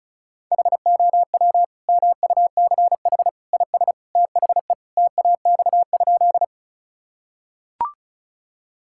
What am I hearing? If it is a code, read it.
Morse code, heard as a keyed tone, about 36 wpm, 700 Hz. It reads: HOW MUCH IS THE TAX?
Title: Sentences from Top 1000 Words - Review Part 5 - 35wpm